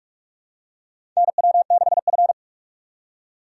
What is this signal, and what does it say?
Morse code, heard as a keyed tone, about 46 wpm, 700 Hz. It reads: NW6F
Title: USA Call Signs - 4 Characters - 3x Character Spacing - 45wpm